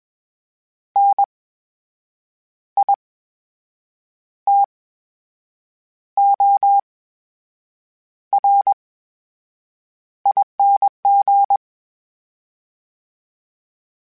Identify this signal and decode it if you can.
Morse code, heard as a keyed tone, about 21 wpm, spaced as for 7 wpm, 800 Hz. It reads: NITORING
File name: Special INGs-mp3s / Special INGs-mp3s / VerySlowToFaster Ing @20